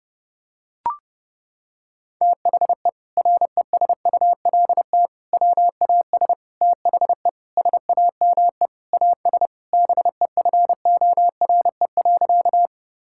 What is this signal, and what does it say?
Morse code, heard as a keyed tone, about 30 wpm, 700 Hz. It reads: THE RESULT WAS THE SAME AS BEFORE.
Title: Sentences from Top 1000 Words - Review Part 5 - 30wpm